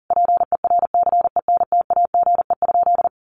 Morse code, heard as a keyed tone, about 40 wpm, 700 Hz. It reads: PERCENTAGE?
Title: Sentences from Top 1300 Words 40wpm